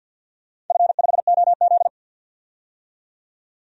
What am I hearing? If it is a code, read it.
Morse code, heard as a keyed tone, about 51 wpm, 700 Hz. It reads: F5OZ
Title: DXCC-100 Call Signs - 4 Characters - 50wpm